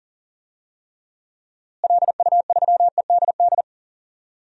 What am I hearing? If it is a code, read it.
Morse code, heard as a keyed tone, about 40 wpm, 700 Hz. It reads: LU3EDD
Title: DXCC-100 Call Signs - 6 Characters - Repeated 2x - 40wpm